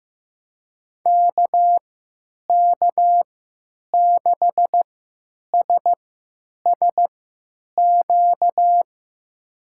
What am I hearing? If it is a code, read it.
Morse code, heard as a keyed tone, about 15 wpm, spaced as for 9 wpm, 700 Hz. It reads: KK6SSQ
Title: USA Call Signs - 6 Characters - 3x Character Spacing - 15wpm